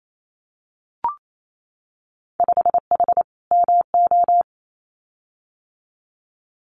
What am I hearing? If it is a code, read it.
Morse code, heard as a keyed tone, about 28 wpm, 700 Hz. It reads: 5H MO
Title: Winter Field Day Exchanges Encore 28wpm